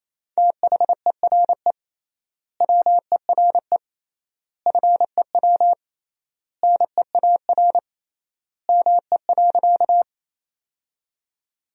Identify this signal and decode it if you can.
Morse code, heard as a keyed tone, about 27 wpm, 700 Hz. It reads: THERE WERE FEW NEAR ME.
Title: 3x Word-Spacing - Sentences from Top 300 Words - 28wpm